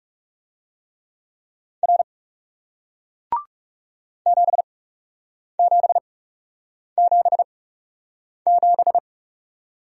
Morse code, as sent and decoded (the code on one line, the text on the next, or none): .-. / --... --... --... --...
R 7777